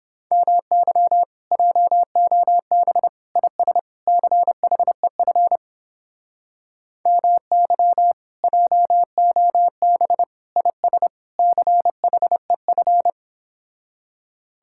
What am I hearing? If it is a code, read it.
Morse code, heard as a keyed tone, about 30 wpm, 700 Hz. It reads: MY JOB IS CHEF MY JOB IS CHEF